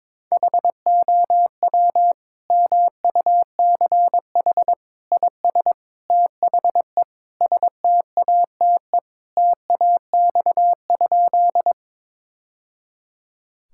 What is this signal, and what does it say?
Morse code, heard as a keyed tone, about 22 wpm, 700 Hz. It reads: HOW MUCH IS THE STATE TAX?
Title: Sentences from Top 1000 Words - Review Part 5 - 22wpm